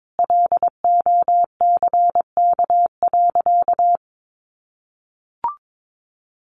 Morse code, heard as a keyed tone, about 22 wpm, 700 Hz. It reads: LOCK.
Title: Sentences from Top 1300 Words 22wpm